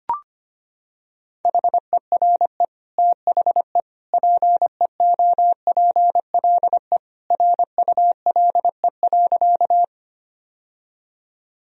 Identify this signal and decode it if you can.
Morse code, heard as a keyed tone, about 25 wpm, 700 Hz. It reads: HERE THE PEOPLE RULE.
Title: Sentences from Top 1000 Words - Review Part 2 - 25wpm